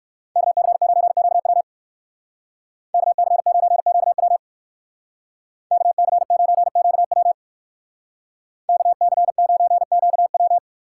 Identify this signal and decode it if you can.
Morse code, heard as a keyed tone, about 54 wpm, 700 Hz. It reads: KC9QW KC9QW KC9QW KC9QW
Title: Speed Racing - USA 2x2 Call Signs 45wpm